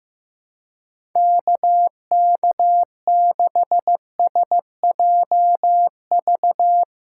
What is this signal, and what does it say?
Morse code, heard as a keyed tone, about 15 wpm, 700 Hz. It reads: KK6SJV
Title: USA Callsign DE Callsign - 6 Characters - 15wpm